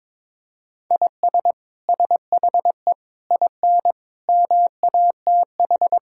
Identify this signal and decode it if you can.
Morse code, heard as a keyed tone, about 21 wpm, 700 Hz. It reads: IS SHE IN MATH
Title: Sentences from Top 1000 Words - Review Part 6 - 22wpm